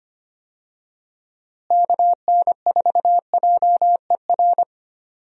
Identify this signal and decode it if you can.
Morse code, heard as a keyed tone, about 25 wpm, 700 Hz. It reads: KN4JER
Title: USA Call Signs - 6 Characters - Repeated 2x - 25wpm